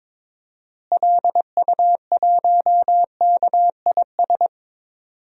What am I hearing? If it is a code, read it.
Morse code, heard as a keyed tone, about 21 wpm, 700 Hz. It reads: LU1KIS
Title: DXCC-100 Call Signs - 6 Characters - Repeated 2x - 22wpm